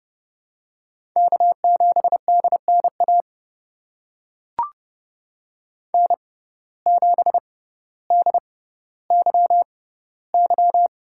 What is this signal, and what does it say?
Morse code, heard as a keyed tone, about 30 wpm, spaced as for 15 wpm, 700 Hz. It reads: K7DNA N7DYY
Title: USA Call Signs - 5 Characters - 6x Character Spacing - 30wpm